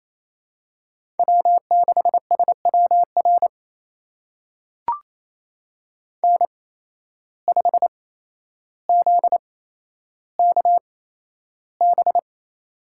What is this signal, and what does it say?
Morse code, heard as a keyed tone, about 28 wpm, spaced as for 12 wpm, 700 Hz. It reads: W6SWR N5ZKB
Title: USA Call Signs - 5 Characters - 8x Character Spacing - 28wpm